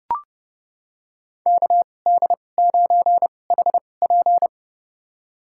Morse code, heard as a keyed tone, about 30 wpm, spaced as for 22 wpm, 700 Hz. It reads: KD9HP
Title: USA Call Signs - 2x Character Spacing - 30wpm